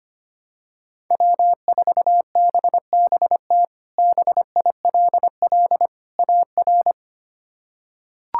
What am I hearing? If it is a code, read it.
Morse code, heard as a keyed tone, about 25 wpm, 700 Hz. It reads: W4BBT BILL AR
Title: K1USN SST QSO Practice - 25wpm